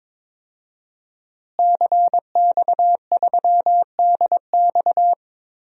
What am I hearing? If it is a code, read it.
Morse code, heard as a keyed tone, about 22 wpm, 700 Hz. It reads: CX3DX
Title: DXCC-100 Call Signs - 5 Characters - Repeated 3x - 22wpm